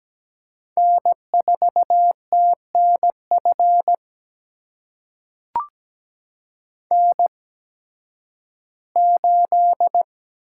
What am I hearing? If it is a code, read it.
Morse code, heard as a keyed tone, about 17 wpm, spaced as for 11 wpm, 700 Hz. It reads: N4TNF N8
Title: USA Call Signs - 5 Characters - 8x Character Spacing - 17wpm